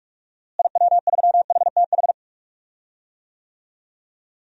Morse code, heard as a keyed tone, about 46 wpm, 700 Hz. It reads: IW3HTH
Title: DXCC-100 Call Signs - 6 Characters - Repeated 3x - 45wpm